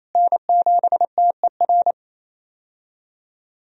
Morse code, heard as a keyed tone, about 27 wpm, 700 Hz. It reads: N7TER
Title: USA Call Signs - 5 Characters - Repeated 3x - 28wpm